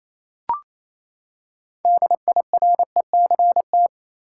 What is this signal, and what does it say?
Morse code, heard as a keyed tone, about 28 wpm, 700 Hz. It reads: DIRECT
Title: Six Letter Words - 28wpm